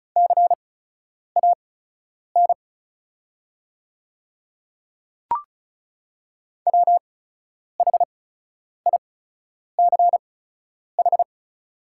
Morse code, heard as a keyed tone, about 36 wpm, spaced as for 10 wpm, 700 Hz. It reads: CAN WHICH
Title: 8x Character Spacing - Top 100 Words - 35wpm